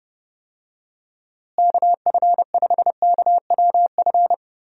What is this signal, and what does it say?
Morse code, heard as a keyed tone, about 31 wpm, 700 Hz. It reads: KF5KWF
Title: USA Call Signs - 6 Characters - Repeated 2x - 30wpm